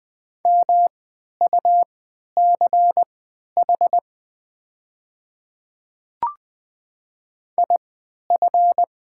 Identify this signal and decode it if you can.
Morse code, heard as a keyed tone, about 20 wpm, spaced as for 11 wpm, 700 Hz. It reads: MUCH IF
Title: 3x Character Spacing - Top 200 Words - 20wpm